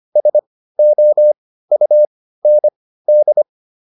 Morse code, heard as a keyed tone, about 25 wpm, spaced as for 15 wpm, 600 Hz. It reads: SOUND